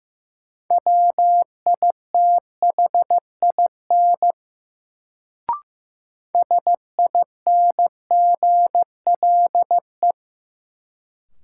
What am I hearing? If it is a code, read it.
Morse code, heard as a keyed tone, about 15 wpm, 700 Hz. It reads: WITHIN SINGLE
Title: Sentences from Top 700 Words - New Word Drill - 15wpm